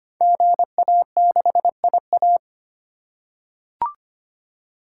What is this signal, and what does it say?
Morse code, heard as a keyed tone, about 25 wpm, 700 Hz. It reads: GA6IA
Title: Code Groups - 5 Letter:Numbers - 25wpm